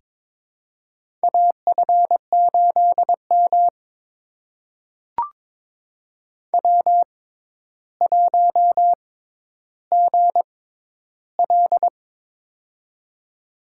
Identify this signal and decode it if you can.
Morse code, heard as a keyed tone, about 22 wpm, spaced as for 11 wpm, 700 Hz. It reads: AF8M W1GL
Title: USA Call Signs - 4 Characters - 6x Character Spacing - 22wpm